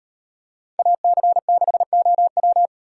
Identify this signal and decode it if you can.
Morse code, heard as a keyed tone, about 36 wpm, 700 Hz. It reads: AC6OW